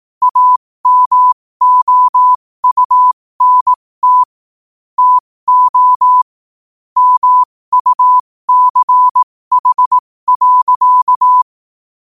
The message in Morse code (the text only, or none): AMOUNT TO MUCH.